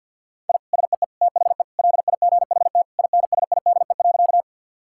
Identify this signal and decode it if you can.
Morse code, heard as a keyed tone, about 52 wpm, 700 Hz. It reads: I SEE THE LIGHT INSIDE.
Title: Sentences from Top 1000 Words - Review Part 6 - 50wpm